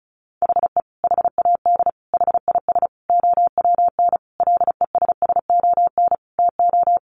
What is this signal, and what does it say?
Morse code, heard as a keyed tone, about 34 wpm, 700 Hz. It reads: HE HAD HIS OWN LESSON TO
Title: Sentences from Top 1200 Words 35wpm